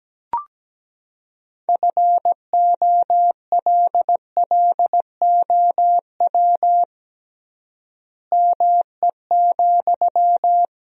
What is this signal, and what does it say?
Morse code, heard as a keyed tone, about 17 wpm, 700 Hz. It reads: FOLLOW ME,
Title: 3x Word-Spacing - Sentences from Top 200 Words - 17wpm